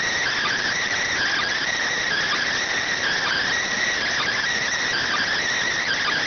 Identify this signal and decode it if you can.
Emergency Alert System (SAME) tones (header bursts), heard as harsh buzzing bursts. It is heard from 1.0 to 5.3 s.